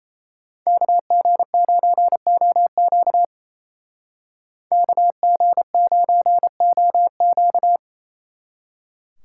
Morse code, heard as a keyed tone, about 33 wpm, 700 Hz. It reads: KG9OQ KG9OQ
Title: Speed Racing - USA 2x2 Call Signs 28wpm